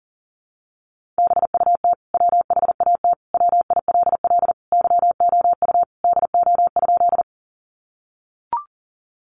Morse code, heard as a keyed tone, about 40 wpm, 700 Hz. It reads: BUT WHAT WILL YOU DO?
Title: Sentences from Top 100 Words 40wpm